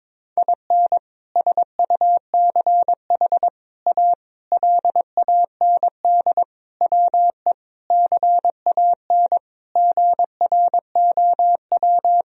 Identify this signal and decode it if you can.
Morse code, heard as a keyed tone, about 21 wpm, 700 Hz. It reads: IN SUCH A LAND WE CAN GROW
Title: Sentences from Top 1000 Words - Review Part 1 - 22wpm